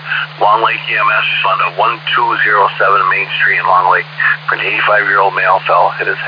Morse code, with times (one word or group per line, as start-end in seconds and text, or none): none